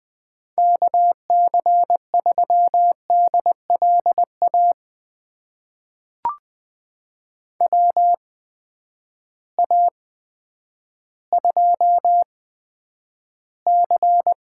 Morse code, heard as a keyed tone, about 20 wpm, spaced as for 10 wpm, 700 Hz. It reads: KC3DLA WA2C
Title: USA Call Signs - 6 Characters - 8x Character Spacing - 20wpm